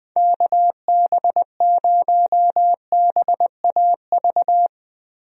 Morse code, heard as a keyed tone, about 21 wpm, 700 Hz. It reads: KB0BAV
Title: USA Call Signs - 6 Characters - Repeated 2x - 20wpm